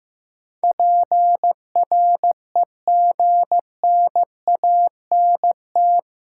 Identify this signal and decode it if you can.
Morse code, heard as a keyed tone, about 15 wpm, 700 Hz. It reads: PREGNANT
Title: Sentences from Top 1000 Words - Review Part 6 - New Word Drill - 15wpm